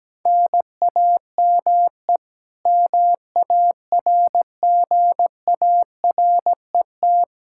Morse code, heard as a keyed tone, about 17 wpm, 700 Hz. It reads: NAME MARGARET
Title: K1USN SST QSO Practice 17wpm